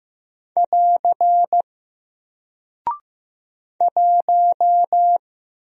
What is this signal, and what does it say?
Morse code, heard as a keyed tone, about 15 wpm, 700 Hz. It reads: <AR> 1
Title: Single Character (Letter:Number:Punctuation:Prosign) 15wpm